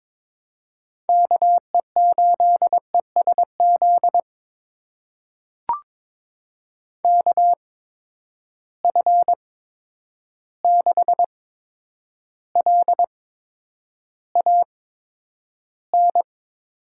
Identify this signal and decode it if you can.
Morse code, heard as a keyed tone, about 22 wpm, spaced as for 9 wpm, 700 Hz. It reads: KE8ESZ KF6LAN
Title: USA Call Signs - 6 Characters - 8x Character Spacing - 22wpm